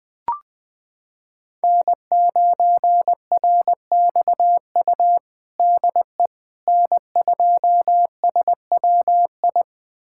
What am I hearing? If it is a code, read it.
Morse code, heard as a keyed tone, about 20 wpm, 700 Hz. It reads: N9RXU DE N2SWI
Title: USA Callsign DE Callsign - 5 Characters - 20wpm